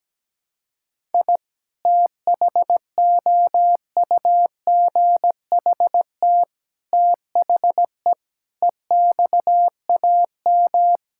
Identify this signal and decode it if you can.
Morse code, heard as a keyed tone, about 17 wpm, 700 Hz. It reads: I THOUGHT THE EXAM
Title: Sentences from Top 1000 Words - Review Part 5 - 17wpm